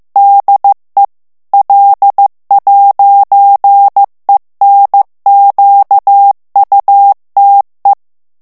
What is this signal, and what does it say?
Morse code, heard as a keyed tone, about 15 wpm, 800 Hz. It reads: DE L'ENQUTE
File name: morse mp3 / FRANCAIS TEXTE CLAIR 1 VITESSE 14 MOTS MINUTES